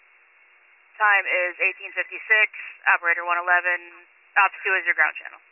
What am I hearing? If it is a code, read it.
Emergency Alert System (SAME) tones (header bursts), heard as harsh buzzing bursts, with speech. It is heard from 0.9 to 5.1 s.